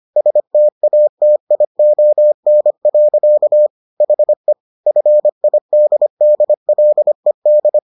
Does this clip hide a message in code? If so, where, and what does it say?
Yes: Morse code from 0.2 to 7.8 s, heard as a keyed tone, about 25 wpm, 600 Hz. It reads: STATION. HE FIDDLED